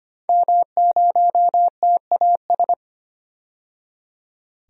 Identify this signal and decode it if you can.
Morse code, heard as a keyed tone, about 25 wpm, 700 Hz. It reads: M0TAS
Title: DXCC-100 Call Signs - 5 Characters - 25wpm